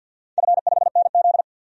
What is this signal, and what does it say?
Morse code, heard as a keyed tone, about 51 wpm, 700 Hz. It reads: F5NZ
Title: DXCC-100 Call Signs - 4 Characters - 50wpm